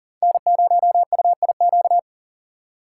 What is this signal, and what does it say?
Morse code, heard as a keyed tone, about 40 wpm, 690 Hz. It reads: N0UIQ
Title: USA Call Signs - 5 Characters - 3x Character Spacing - 40wpm